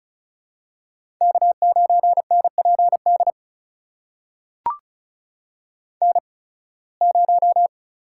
Morse code, heard as a keyed tone, about 36 wpm, spaced as for 23 wpm, 700 Hz. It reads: K9NPD N0
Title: USA Call Signs - 5 Characters - 8x Character Spacing - 35wpm